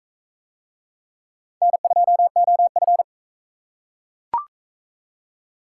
Morse code, heard as a keyed tone, about 34 wpm, 700 Hz. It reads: N2OF
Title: Speed Racing - USA 1x2 Call Signs 28wpm